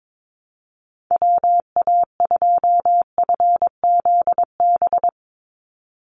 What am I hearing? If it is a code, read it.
Morse code, heard as a keyed tone, about 22 wpm, 700 Hz. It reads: WA2FZB
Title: Call Signs - 22wpm